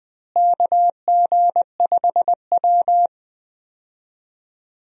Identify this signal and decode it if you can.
Morse code, heard as a keyed tone, about 20 wpm, 700 Hz. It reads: KG5W
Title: USA Call Signs - 4 Characters - Repeated 2x - 20wpm